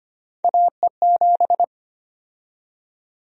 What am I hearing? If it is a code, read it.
Morse code, heard as a keyed tone, about 25 wpm, 700 Hz. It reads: AE7
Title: USA Call Sign Prefixes - 25wpm